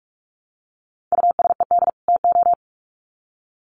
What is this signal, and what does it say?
Morse code, heard as a keyed tone, about 44 wpm, 700 Hz. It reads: USED TO
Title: Sets of 2 Words 45wpm